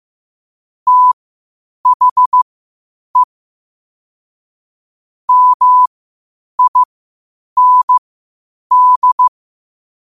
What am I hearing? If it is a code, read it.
Morse code, heard as a keyed tone, about 15 wpm, spaced as for 8 wpm, 1000 Hz. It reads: THE MIND
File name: qotdfiles / 8wpm